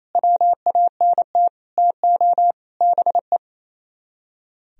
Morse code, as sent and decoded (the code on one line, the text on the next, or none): .-- .- -. - / - --- / -... .
WANT TO BE